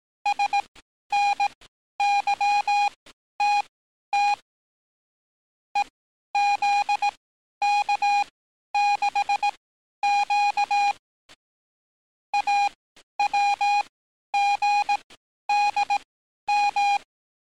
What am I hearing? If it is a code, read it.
Morse code, heard as a keyed tone, about 18 wpm, spaced as for 11 wpm, 820 Hz. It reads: SNYTT EZK6Q AWGDM
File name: training / NAVPERS 11049to11060 / NAVPERS 11050 RA RB